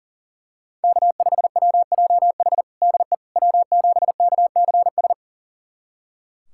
Morse code, heard as a keyed tone, about 40 wpm, 700 Hz. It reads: K5WJH DE W7KCS
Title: USA Callsign DE Callsign - 5 Characters - 40wpm